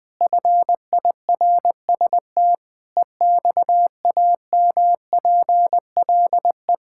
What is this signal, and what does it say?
Morse code, heard as a keyed tone, about 20 wpm, 700 Hz. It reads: FIRST EXAMPLE
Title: Sentences with 6-Word Phrases - 20wpm